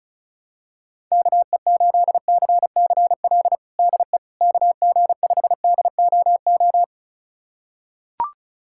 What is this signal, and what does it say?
Morse code, heard as a keyed tone, about 34 wpm, 700 Hz. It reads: KE8CCL DE KG5DOO
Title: USA Callsign DE Callsign - 6 Characters - 35wpm